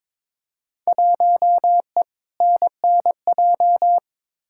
Morse code, heard as a keyed tone, about 21 wpm, 700 Hz. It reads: 1E NNJ
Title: Field Day Exchanges - 22wpm